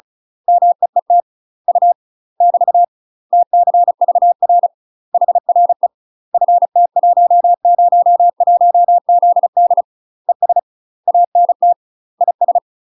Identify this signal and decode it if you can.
Morse code, heard as a keyed tone, about 36 wpm, 700 Hz. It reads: MEET U <BT> TCVR HRE FT101ZD ES ANT IS
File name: UK MASTER QSOs TRAINING / 35WPM / UK MASTER QSOs 0018